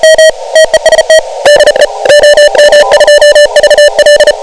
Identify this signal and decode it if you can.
Morse code, heard as a keyed tone, about 32 wpm, 1750 Hz, with noise. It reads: DE OM2VL
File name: om7m / OM7M a OM2VL 28MHz 1500Z